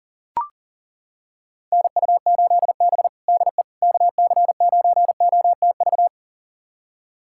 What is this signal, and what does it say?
Morse code, heard as a keyed tone, about 40 wpm, 700 Hz. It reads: NU8B DE KC9OTV
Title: USA Callsign DE Callsign 40wpm